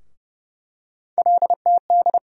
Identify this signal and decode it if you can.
Morse code, heard as a keyed tone, about 30 wpm, 700 Hz. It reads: LTD
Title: USA Call Sign Postfixes - 3 Letters 30wpm